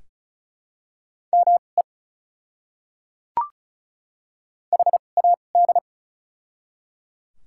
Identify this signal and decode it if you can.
Morse code, heard as a keyed tone, about 36 wpm, spaced as for 25 wpm, 700 Hz. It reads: ME HAD
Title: 2x Character Spacing - Top 200 Words - 35wpm